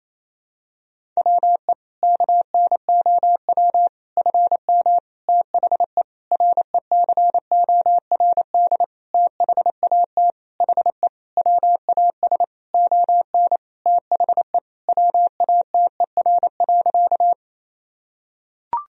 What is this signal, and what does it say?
Morse code, heard as a keyed tone, about 28 wpm, 700 Hz. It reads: WE KNOW FM THE RECORD THAT HE WAS ON THE WATER.
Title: Sentences from Top 500 Words and Abbreviations 28wpm